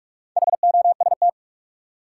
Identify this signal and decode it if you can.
Morse code, heard as a keyed tone, about 46 wpm, 700 Hz. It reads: HOST
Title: Four Letter Words - 45wpm